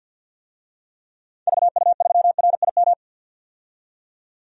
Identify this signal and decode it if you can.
Morse code, heard as a keyed tone, about 52 wpm, 700 Hz. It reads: VU3RIM
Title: DXCC-100 Call Signs - 6 Characters - 50wpm